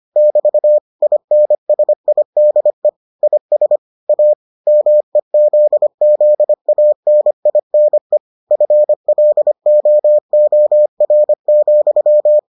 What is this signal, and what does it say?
Morse code, heard as a keyed tone, about 25 wpm, 600 Hz. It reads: <BT> INSIDE IS A MEZZANINE FLOOR,